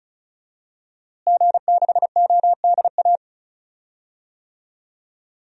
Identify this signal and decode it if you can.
Morse code, heard as a keyed tone, about 36 wpm, 700 Hz. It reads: G6ODA
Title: DXCC-100 Call Signs - 5 Characters - Repeated 2x - 35wpm